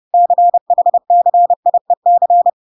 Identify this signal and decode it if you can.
Morse code, heard as a keyed tone, about 30 wpm, 700 Hz. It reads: CHCIEC